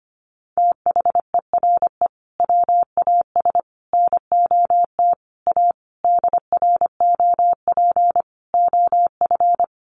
Morse code, heard as a keyed tone, about 25 wpm, 700 Hz. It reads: THERE WAS NOT A DROP OF
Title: Sentences from Top 1500 Words 25wpm